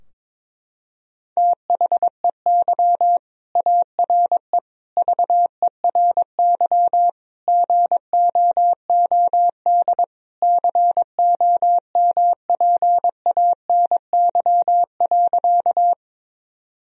Morse code, heard as a keyed tone, about 22 wpm, 700 Hz. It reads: THEY ARE VERY GOOD COMPANY.
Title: Sentences from Top 1000 Words - Review Part 3 - 22wpm